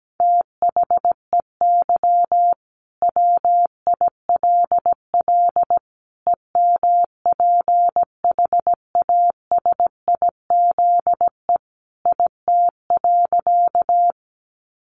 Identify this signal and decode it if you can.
Morse code, heard as a keyed tone, about 17 wpm, 700 Hz. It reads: THEY WILL EMPHASIZE IT.
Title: Sentences from Top 1500 Words 17wpm